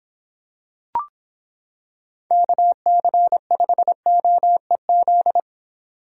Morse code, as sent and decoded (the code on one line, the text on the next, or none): -.- -.-. ..... --- . --..
KC5OEZ